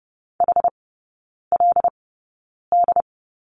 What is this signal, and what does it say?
Morse code, heard as a keyed tone, about 30 wpm, spaced as for 9 wpm, 700 Hz. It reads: HLD
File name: 081 Lesson 14e - Number 2 - Three Characters 30wpm